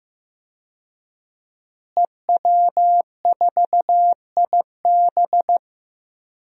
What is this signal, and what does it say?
Morse code, heard as a keyed tone, about 16 wpm, 700 Hz. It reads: EW4IB
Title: DXCC-100 Call Signs - 5 Characters - Repeated 3x - 15wpm